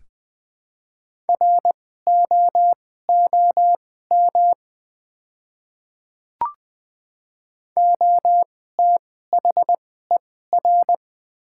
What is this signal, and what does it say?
Morse code, heard as a keyed tone, about 20 wpm, spaced as for 14 wpm, 700 Hz. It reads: ROOM OTHER
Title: 2x Character Spacing - Top 300 Words - 20wpm